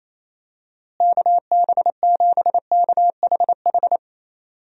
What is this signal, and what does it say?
Morse code, heard as a keyed tone, about 28 wpm, 700 Hz. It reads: KB7KHH
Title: USA Call Signs - 6 Characters - Repeated 2x - 28wpm